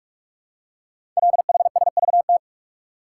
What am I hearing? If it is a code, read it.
Morse code, heard as a keyed tone, about 44 wpm, 690 Hz. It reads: LHSVT
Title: Code Groups - 5 Letters - 45wpm